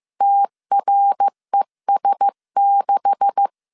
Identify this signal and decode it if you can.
Morse code, heard as a keyed tone, about 15 wpm, 800 Hz. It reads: TRES6
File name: CW / DET / 1 9 1500C S B 14wpm 3302